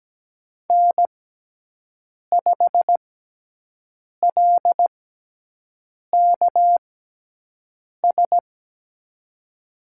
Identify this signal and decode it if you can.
Morse code, heard as a keyed tone, about 17 wpm, spaced as for 6 wpm, 700 Hz. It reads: N5LKS
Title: USA Call Signs - 6x Character Spacing - 17wpm